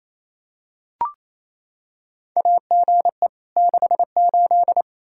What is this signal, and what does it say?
Morse code, heard as a keyed tone, about 28 wpm, 700 Hz. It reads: AGE 68
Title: Practice QSO - Exchange 3 - More Info - 28wpm